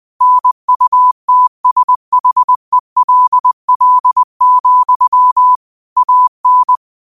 Morse code, heard as a keyed tone, about 20 wpm, 1000 Hz. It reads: NUTSHELL, AN
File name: qotdfiles / 20wpm